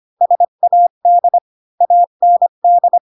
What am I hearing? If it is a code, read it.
Morse code, heard as a keyed tone, about 25 wpm, 700 Hz. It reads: SAD AND
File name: Pinocchio / Pinocchio ch3 20 25 ik1hge